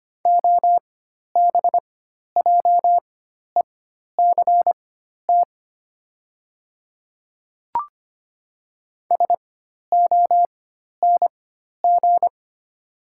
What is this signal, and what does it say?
Morse code, heard as a keyed tone, about 25 wpm, spaced as for 12 wpm, 700 Hz. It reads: OBJECT SONG
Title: 4x Character Spacing - Top 500 Words - 25wpm